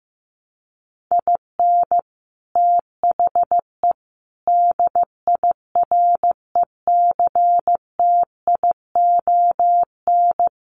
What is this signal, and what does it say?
Morse code, heard as a keyed tone, about 15 wpm, 700 Hz. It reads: IN THE DIRECTION
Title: Sets of 4 Words 15wpm